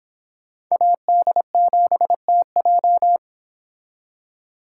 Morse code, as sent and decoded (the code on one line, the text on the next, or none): .- -.. --... - .---
AD7TJ